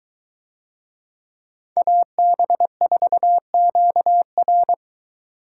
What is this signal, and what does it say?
Morse code, heard as a keyed tone, about 21 wpm, 700 Hz. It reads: AB4QR